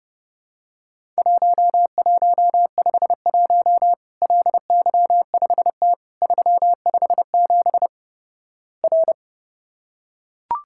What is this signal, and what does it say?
Morse code, heard as a keyed tone, about 29 wpm, 700 Hz. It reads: 1151 LY5T 357 R
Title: Worked All Europe DX Contest - QTC Exchange - 30wpm